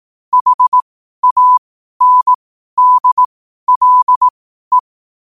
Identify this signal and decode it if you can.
Morse code, heard as a keyed tone, about 18 wpm, spaced as for 12 wpm, 1000 Hz. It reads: HANDLE
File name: qotdfiles / 12wpm